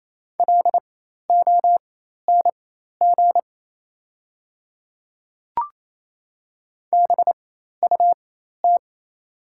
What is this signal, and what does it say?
Morse code, heard as a keyed tone, about 28 wpm, spaced as for 13 wpm, 690 Hz. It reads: LONG BUT
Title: 4x Character Spacing - Top 100 Words - 28wpm